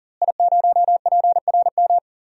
Morse code, heard as a keyed tone, about 40 wpm, 700 Hz. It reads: I0PRM
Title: DXCC-100 Call Signs - 5 Characters - Repeated 3x - 40wpm